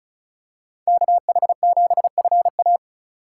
Morse code, heard as a keyed tone, about 36 wpm, 700 Hz. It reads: KH7FA
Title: USA Call Signs - 5 Characters - 3x Character Spacing - 35wpm